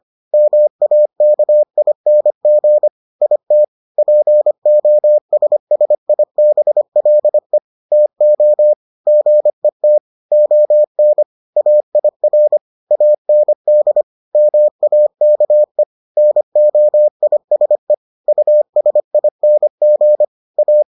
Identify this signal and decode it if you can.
Morse code, heard as a keyed tone, about 25 wpm, 600 Hz. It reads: MAKING IT POSSIBLE TO GET ON AIR AND MAKE NOISE USING A